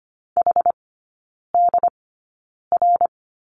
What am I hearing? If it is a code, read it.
Morse code, heard as a keyed tone, about 25 wpm, spaced as for 9 wpm, 700 Hz. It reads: HDR
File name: course / 25:10wpm / 093 Lesson 16e - Letter c - Three Characters 25wpm